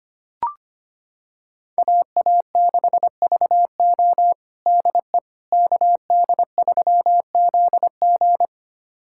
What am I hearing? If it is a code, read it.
Morse code, heard as a keyed tone, about 25 wpm, 700 Hz. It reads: AA6VO DE KD3ZG
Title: USA Callsign DE Callsign - 5 Characters - 25wpm